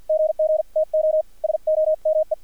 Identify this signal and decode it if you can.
Morse code, heard as a keyed tone, about 50 wpm, spaced as for 42 wpm, 630 Hz, with radio noise. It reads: GGTO SOME
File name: archived / CRSnet